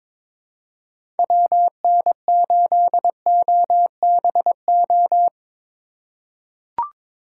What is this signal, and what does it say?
Morse code, heard as a keyed tone, about 22 wpm, 700 Hz. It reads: WN8OBO
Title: USA Call Signs - 6 Characters - 3x Character Spacing - 22wpm